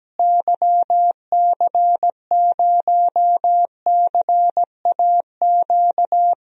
Morse code, heard as a keyed tone, about 17 wpm, 700 Hz. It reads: YC0CAQ
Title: DXCC-100 Call Signs - 6 Characters - Repeated 2x - 17wpm